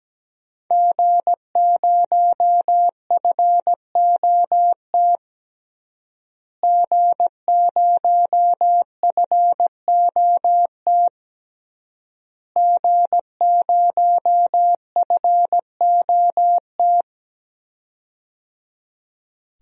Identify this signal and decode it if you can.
Morse code, heard as a keyed tone, about 17 wpm, 700 Hz. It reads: G0FOT G0FOT G0FOT
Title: DXCC-100 Call Signs - 5 Characters - Repeated 3x - 17wpm